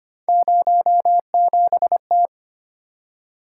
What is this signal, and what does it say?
Morse code, heard as a keyed tone, about 25 wpm, 700 Hz. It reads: 07T
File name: Three Letter-Number Combinations - 25wpm